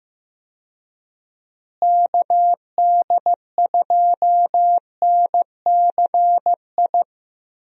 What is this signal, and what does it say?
Morse code, heard as a keyed tone, about 16 wpm, 700 Hz. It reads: KD2NCI
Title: USA Call Signs - 6 Characters - Repeated 2x - 15wpm